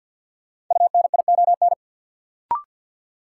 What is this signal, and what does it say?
Morse code, heard as a keyed tone, about 51 wpm, 700 Hz. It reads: UNION
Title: Sets of 5 Words Encore - New Word Drill - 50wpm